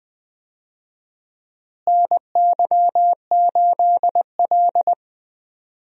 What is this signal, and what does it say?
Morse code, heard as a keyed tone, about 20 wpm, 700 Hz. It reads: NY8L
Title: USA Call Signs - 4 Characters - Repeated 2x - 20wpm